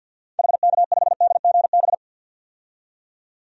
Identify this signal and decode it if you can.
Morse code, heard as a keyed tone, about 48 wpm, 700 Hz. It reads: HK5DGB
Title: DXCC-100 Call Signs - 6 Characters - 50wpm